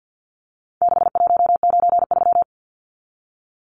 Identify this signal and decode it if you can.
Morse code, heard as a keyed tone, about 50 wpm, 700 Hz. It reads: 6193
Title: Numbers - 4-Digit 50wpm